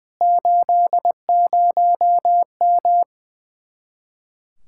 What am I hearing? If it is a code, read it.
Morse code, heard as a keyed tone, about 20 wpm, 700 Hz. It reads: 80M
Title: Speed-Racing Common QSO Phrases - 20wpm